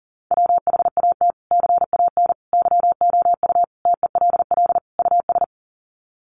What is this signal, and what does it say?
Morse code, heard as a keyed tone, about 40 wpm, 700 Hz. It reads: WHAT CAN YOU TELL US
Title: Sets of 5 Words Encore 40wpm